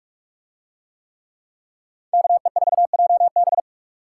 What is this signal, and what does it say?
Morse code, heard as a keyed tone, about 44 wpm, 700 Hz. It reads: KE4JB